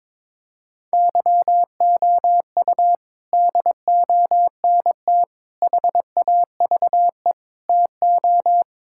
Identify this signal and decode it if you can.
Morse code, heard as a keyed tone, about 22 wpm, 700 Hz. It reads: YOU DONT HAVE TO
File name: practice / 22wpm / Sets of 8 Words - 22wpm